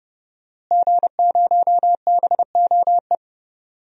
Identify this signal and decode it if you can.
Morse code, heard as a keyed tone, about 30 wpm, 700 Hz. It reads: G0BOE
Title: DXCC-100 Call Signs - 5 Characters - Repeated 2x - 30wpm